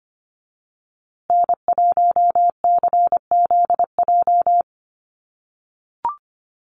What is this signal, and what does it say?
Morse code, heard as a keyed tone, about 25 wpm, 700 Hz. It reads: N1CZJ
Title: USA Call Signs Repeated 3x 25wpm